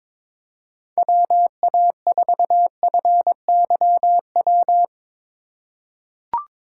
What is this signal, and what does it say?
Morse code, heard as a keyed tone, about 22 wpm, 700 Hz. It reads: WA4FYW
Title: USA Call Signs - 3x Character Spacing - 22wpm